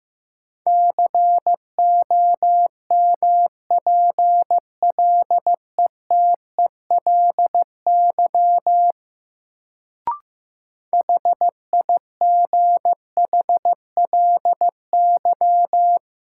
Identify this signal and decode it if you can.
Morse code, heard as a keyed tone, about 15 wpm, 700 Hz. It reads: COMPLETELY HIGHLY
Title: Sentences from Top 1100 Words - New Word Drill - 15wpm